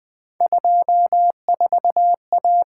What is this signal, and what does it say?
Morse code, heard as a keyed tone, about 24 wpm, 700 Hz. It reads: 24A
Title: Speed-Racing - Three Letter-Number Combinations - 20wpm